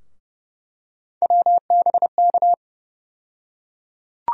Morse code, heard as a keyed tone, about 30 wpm, 700 Hz. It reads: WBK
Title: USA Call Sign Postfixes - 3 Letters 30wpm